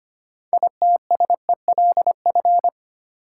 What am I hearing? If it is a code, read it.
Morse code, heard as a keyed tone, about 25 wpm, 700 Hz. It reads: ITSELF
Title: Sentences from Top 800 Words - New Word Drill - 25wpm